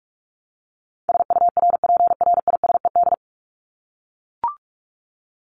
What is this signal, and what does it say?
Morse code, heard as a keyed tone, about 44 wpm, 700 Hz. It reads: SURPRISED
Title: Top 1501-1750 Words 45wpm